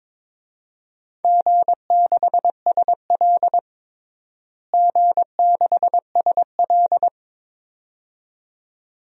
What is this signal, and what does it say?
Morse code, heard as a keyed tone, about 22 wpm, 700 Hz. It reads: G6SL G6SL
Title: DXCC-100 Call Signs - 4 Characters - Repeated 2x - 22wpm